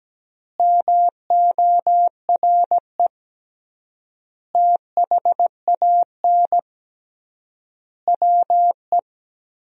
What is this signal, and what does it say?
Morse code, heard as a keyed tone, about 17 wpm, 700 Hz. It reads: MORE THAN WE
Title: 3x Word-Spacing - Sentences from Top 100 Words - 17wpm